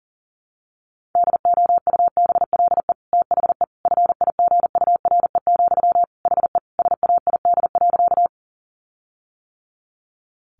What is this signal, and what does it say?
Morse code, heard as a keyed tone, about 37 wpm, 700 Hz. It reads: DOUBLE THE FIGURE, HE SAID.
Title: Sentences from Top 1400 Words 40wpm